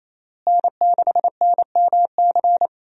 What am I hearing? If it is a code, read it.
Morse code, heard as a keyed tone, about 28 wpm, 700 Hz. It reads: N6NMC
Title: USA Call Signs - 5 Characters - Repeated 2x - 28wpm